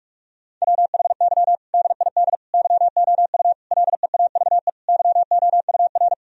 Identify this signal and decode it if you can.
Morse code, heard as a keyed tone, about 46 wpm, 700 Hz. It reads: WHY DID YOU LEAVE YOUR
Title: Sentences from Top 1000 Words - Review Part 3 - 45wpm